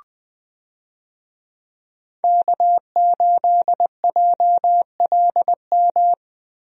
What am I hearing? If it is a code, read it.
Morse code, heard as a keyed tone, about 20 wpm, 700 Hz. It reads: K8JLM
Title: USA Call Signs - 5 Characters - Repeated 2x - 20wpm